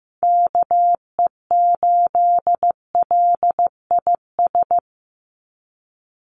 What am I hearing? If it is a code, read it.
Morse code, heard as a keyed tone, about 15 wpm, 700 Hz. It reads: KE8LIS
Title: USA Call Signs 15wpm